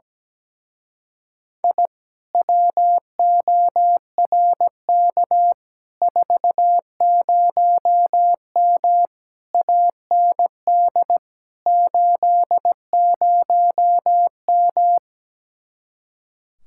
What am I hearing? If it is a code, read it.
Morse code, heard as a keyed tone, about 17 wpm, 700 Hz. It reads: I WORK 40M AND 80M
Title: Speed-Racing Common QSO Phrases - 17wpm